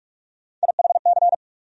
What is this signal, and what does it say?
Morse code, heard as a keyed tone, about 42 wpm, 700 Hz. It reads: IHC
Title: USA Call Sign Postfixes - 3 Letters 45wpm